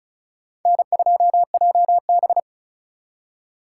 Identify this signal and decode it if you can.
Morse code, heard as a keyed tone, about 36 wpm, 700 Hz. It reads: N2JB
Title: USA Call Signs - 2x Character Spacing - 35wpm